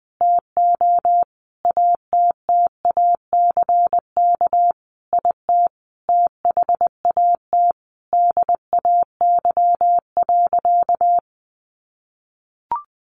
Morse code, heard as a keyed tone, about 20 wpm, 700 Hz. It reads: TO ATTACK IT THAT DAY.
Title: Sentences from Top 1500 Words 20wpm